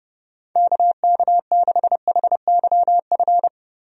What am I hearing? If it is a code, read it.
Morse code, heard as a keyed tone, about 30 wpm, 700 Hz. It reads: KK6HYF
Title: USA Call Signs - 6 Characters - Repeated 3x - 30wpm